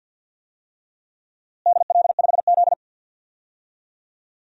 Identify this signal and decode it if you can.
Morse code, heard as a keyed tone, about 52 wpm, 700 Hz. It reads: DL5Z
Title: DXCC-100 Call Signs - 4 Characters - Repeated 2x - 50wpm